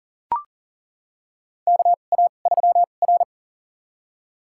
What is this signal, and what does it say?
Morse code, heard as a keyed tone, about 40 wpm, spaced as for 29 wpm, 700 Hz. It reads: KA3R
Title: USA Call Signs - 2x Character Spacing - 40wpm